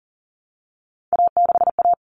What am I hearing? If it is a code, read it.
Morse code, heard as a keyed tone, about 40 wpm, 700 Hz. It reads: A6A
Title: Common Cut Numbers - 3-Digit 40wpm